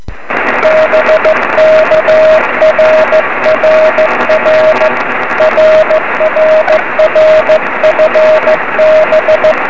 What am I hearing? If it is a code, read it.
Morse code, heard as a keyed tone, about 15 wpm, 590 Hz, with radio noise. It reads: BK RRR RRR FB
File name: 2108z pa3awj 23cm